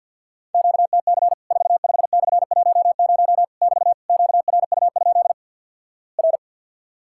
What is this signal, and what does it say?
Morse code, heard as a keyed tone, about 50 wpm, 700 Hz. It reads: QTC 45/10 <BT> QRU? R
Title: Worked All Europe DX Contest - QTC Exchange - 50wpm